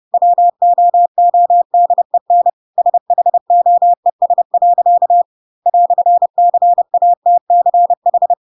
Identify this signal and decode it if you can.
Morse code, heard as a keyed tone, about 30 wpm, 700 Hz. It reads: WOODEN SHOES. "CATCH
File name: Pinocchio / Pinocchio ch3 30 30 ik1hge